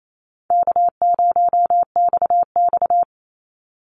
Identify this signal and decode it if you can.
Morse code, heard as a keyed tone, about 28 wpm, 700 Hz. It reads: K0XX
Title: Easy Call Signs Repeated 2x - 28wpm